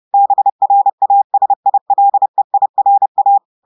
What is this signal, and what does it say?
Morse code, heard as a keyed tone, about 30 wpm, 800 Hz. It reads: BRASILEIRA
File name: audios / portugais / portugais texte en clair 1 vitesse 30 mots minutes